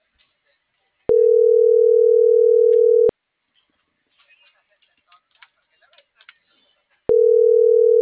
Ringback tone (North America), 440 + 480 Hz.